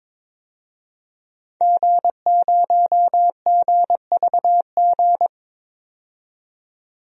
Morse code, heard as a keyed tone, about 22 wpm, 700 Hz. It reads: G0GVG
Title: DXCC-100 Call Signs - 5 Characters - 22wpm